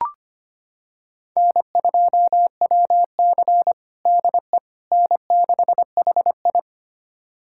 Morse code, heard as a keyed tone, about 25 wpm, 700 Hz. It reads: N2WC DE N6HI
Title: USA Callsign DE Callsign - 4 Characters - 25wpm